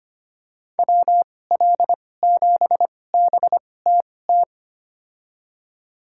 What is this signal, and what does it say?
Morse code, heard as a keyed tone, about 25 wpm, spaced as for 18 wpm, 700 Hz. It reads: WL7BTT
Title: USA Call Signs - 6 Characters - 2x Character Spacing - 25wpm